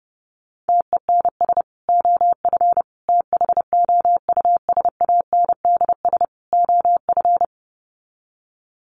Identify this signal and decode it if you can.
Morse code, heard as a keyed tone, about 28 wpm, 700 Hz. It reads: TENS OF THOUSANDS OF
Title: Sets of 4 Words 30wpm